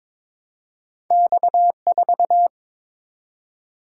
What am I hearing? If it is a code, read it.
Morse code, heard as a keyed tone, about 22 wpm, 700 Hz. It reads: X4
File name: practice / 22wpm / Two Letter-Number Combinations - 22wpm